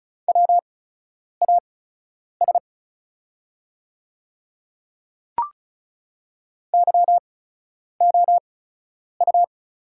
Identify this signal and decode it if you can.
Morse code, heard as a keyed tone, about 34 wpm, spaced as for 10 wpm, 690 Hz. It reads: WAS YOU
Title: 8x Character Spacing - Top 100 Words - 35wpm